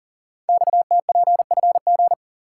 Morse code, heard as a keyed tone, about 42 wpm, 700 Hz. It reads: XTPFG
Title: Code Groups - 5 Letters - 40wpm